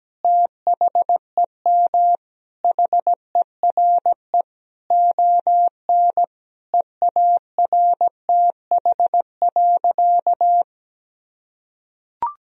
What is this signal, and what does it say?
Morse code, heard as a keyed tone, about 17 wpm, 700 Hz. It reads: THEM HERE ON EARTH.
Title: Sentences from Top 1000 Words - Review Part 2 - 17wpm